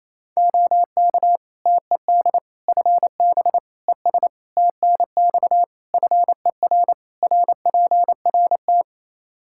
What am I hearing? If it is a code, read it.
Morse code, heard as a keyed tone, about 28 wpm, 700 Hz. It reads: OK TED FB ES TNX FER RPRT
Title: Practice QSO - Exchange 2 - Info - 28wpm